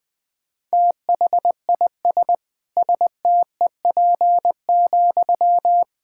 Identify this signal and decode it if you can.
Morse code, heard as a keyed tone, about 21 wpm, 700 Hz. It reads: THIS STEP,
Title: Sentences from Top 1000 Words - Review Part 6 - 20wpm